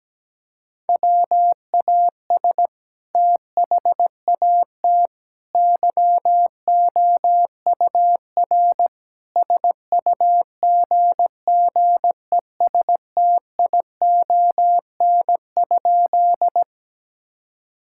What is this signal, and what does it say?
Morse code, heard as a keyed tone, about 17 wpm, 700 Hz. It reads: WAS THAT YOUR SUGGESTION?
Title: Sentences from Top 1600 Words 17wpm